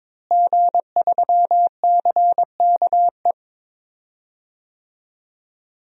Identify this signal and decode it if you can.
Morse code, heard as a keyed tone, about 22 wpm, 700 Hz. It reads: G3CKE
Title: DXCC-100 Call Signs - 5 Characters - Repeated 2x - 22wpm